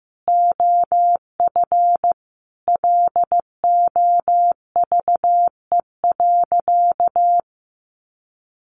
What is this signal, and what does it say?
Morse code, heard as a keyed tone, about 15 wpm, 700 Hz. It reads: OF LOVE.
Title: Sentences from Top 1300 Words 15wpm